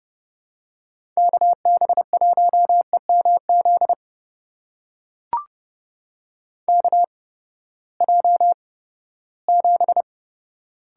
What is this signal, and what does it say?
Morse code, heard as a keyed tone, about 32 wpm, spaced as for 17 wpm, 700 Hz. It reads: KB1EMZ KJ7
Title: USA Call Signs - 6 Characters - 8x Character Spacing - 30wpm